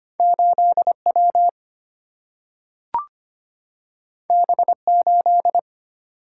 Morse code, heard as a keyed tone, about 25 wpm, 700 Hz. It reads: 8W B8
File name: Two Letter-Number Combinations - 25wpm